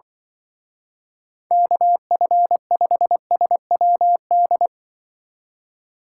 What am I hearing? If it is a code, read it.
Morse code, heard as a keyed tone, about 24 wpm, 700 Hz. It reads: KF5SWD
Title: Speed Racing - USA 2x3 Call Signs 20wpm